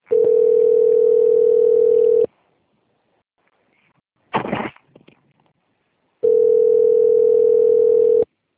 Ringback tone (North America), 440 + 480 Hz.
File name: out- - - - - .45260